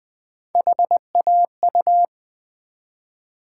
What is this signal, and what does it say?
Morse code, heard as a keyed tone, about 20 wpm, 700 Hz. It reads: HAU
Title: Build A Word - 1 Letter At A Time - 3 to 5-Letter Words - 20wpm